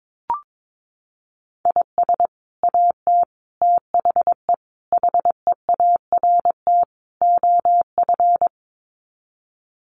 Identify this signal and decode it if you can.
Morse code, heard as a keyed tone, about 22 wpm, 700 Hz. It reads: IS AT THE HEART OF
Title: Sets of 5 Words Encore2 22wpm